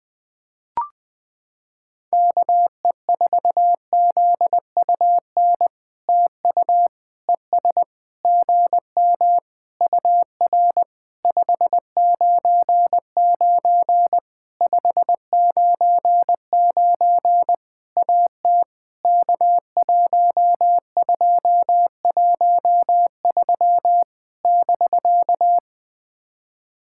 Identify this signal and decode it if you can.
Morse code, heard as a keyed tone, about 20 wpm, 700 Hz. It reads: KE4ZUN TU ES GM UR 599 599 AT K1213 <BK>
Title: POTA QSO Practice - 20wpm